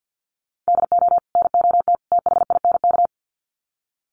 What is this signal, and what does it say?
Morse code, heard as a keyed tone, about 50 wpm, 700 Hz. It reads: DO NOT THINK
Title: Sets of 3 Words 50wpm